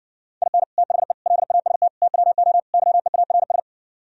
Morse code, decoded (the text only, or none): IN THE LAST TWO YEARS